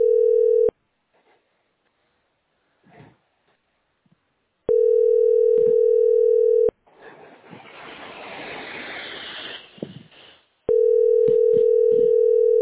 Ringback tone (North America), 440 + 480 Hz.